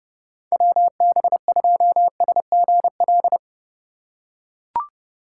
Morse code, heard as a keyed tone, about 30 wpm, 700 Hz. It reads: WB2SGL
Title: USA Call Signs - 6 Characters - 2x Character Spacing - 30wpm